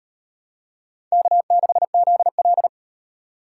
Morse code, heard as a keyed tone, about 35 wpm, 700 Hz. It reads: K6ZL